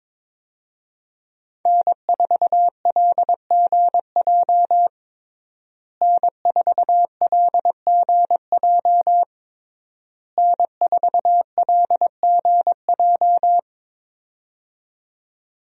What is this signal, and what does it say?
Morse code, heard as a keyed tone, about 22 wpm, 700 Hz. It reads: N4LGJ N4LGJ N4LGJ
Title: USA Call Signs - 5 Characters - Repeated 3x - 22wpm